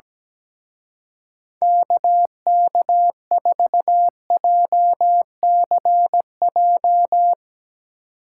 Morse code, heard as a keyed tone, about 17 wpm, 700 Hz. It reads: KK4JCJ
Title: Speed Racing - USA 2x3 Call Signs 17wpm